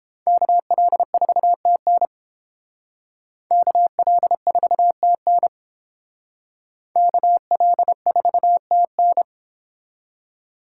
Morse code, heard as a keyed tone, about 29 wpm, 700 Hz. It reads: KL4TN KL4TN KL4TN